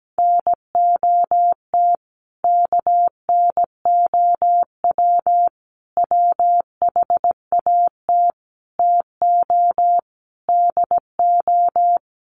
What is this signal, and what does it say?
Morse code, heard as a keyed tone, about 17 wpm, 700 Hz. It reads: NOT KNOW WHAT TO DO
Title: Sets of 5 Words 17wpm